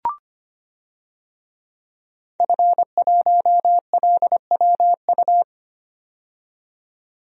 Morse code, heard as a keyed tone, about 25 wpm, 700 Hz. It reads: F1LWU
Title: DXCC-100 Call Signs - 5 Characters - 25wpm